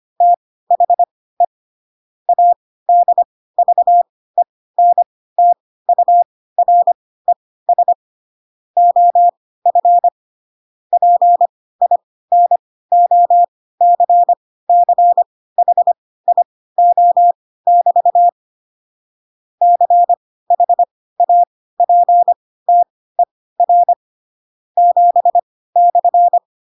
Morse code, heard as a keyed tone, about 25 wpm, spaced as for 16 wpm, 700 Hz. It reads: THE ADVENTURES OF PINOCCHIO<BT> CHAPTER 7/
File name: Pinocchio / Pinocchio ch7 10 25 ik1hge